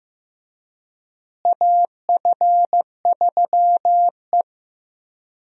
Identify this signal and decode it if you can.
Morse code, heard as a keyed tone, about 16 wpm, 700 Hz. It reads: AF3E
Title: USA Call Signs - 4 Characters - Repeated 3x - 15wpm